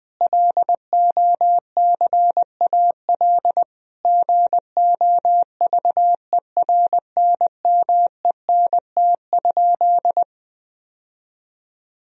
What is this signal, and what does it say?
Morse code, heard as a keyed tone, about 20 wpm, 700 Hz. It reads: LOCAL GOVERNMENT?
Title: Sentences from Top 1000 Words - Review Part 4 - 20wpm